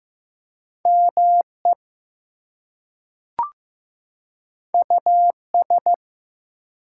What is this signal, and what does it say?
Morse code, heard as a keyed tone, about 15 wpm, 700 Hz. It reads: ME US
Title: Two Letter Words - 15wpm